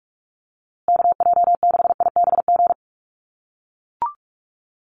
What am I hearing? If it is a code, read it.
Morse code, heard as a keyed tone, about 44 wpm, 700 Hz. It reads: KJ6IBG
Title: USA Call Signs Repeated 2x 45wpm